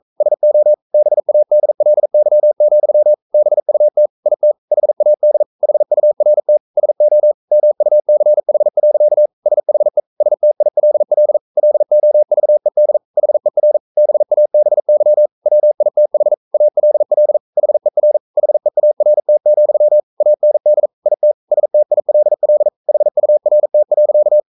Morse code, heard as a keyed tone, about 42 wpm, 600 Hz. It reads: SO BADLY, BUT IT HAD HURT SO MUCH. SHE STILL LOVED HER BABY WITH ALL HER HEART, AND IT STILL HURT.
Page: Romanen Never af Ken Follet på engelsk.